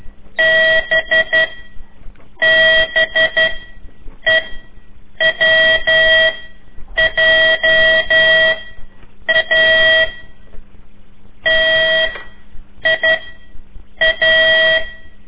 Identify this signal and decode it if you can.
Morse code, heard as a keyed tone, about 12 wpm, 1950 Hz. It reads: B B E W J A T I A
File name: ik1weg2 / CORSI CW / lez4-3